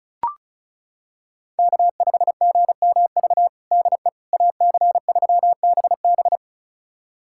Morse code, heard as a keyed tone, about 36 wpm, 700 Hz. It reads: K5GMV DE AC3BB
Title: USA Callsign DE Callsign - 5 Characters - 35wpm